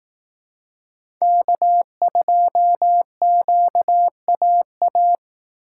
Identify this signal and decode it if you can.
Morse code, heard as a keyed tone, about 19 wpm, 700 Hz. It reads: K2QAA
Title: Speed Racing - USA 1x3 Call Signs 15wpm